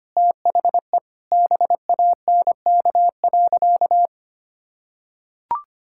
Morse code, heard as a keyed tone, about 25 wpm, 700 Hz. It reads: THE BANK.
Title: Sentences from Top 1000 Words - Review Part 6 - 25wpm